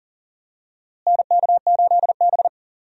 Morse code, heard as a keyed tone, about 40 wpm, 700 Hz. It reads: NK8B
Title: USA Call Signs - 4 Characters - Repeated 2x - 40wpm